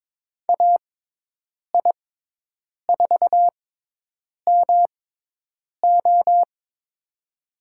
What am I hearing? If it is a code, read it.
Morse code, heard as a keyed tone, about 22 wpm, spaced as for 8 wpm, 700 Hz. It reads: AI4MO
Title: USA Call Signs - 5 Characters - 6x Character Spacing - 22wpm